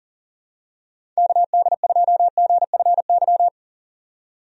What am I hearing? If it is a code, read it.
Morse code, heard as a keyed tone, about 40 wpm, 700 Hz. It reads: KD2GFY
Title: USA Call Signs - 6 Characters - 40wpm